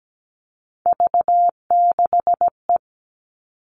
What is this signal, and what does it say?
Morse code, heard as a keyed tone, about 17 wpm, 700 Hz. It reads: V6E
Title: All Cut Numbers - 3-Digit 17wpm